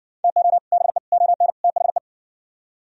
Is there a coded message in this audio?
Yes: Morse code from 0.2 to 2.0 s, heard as a keyed tone, about 58 wpm, 700 Hz. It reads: TO BE ON THE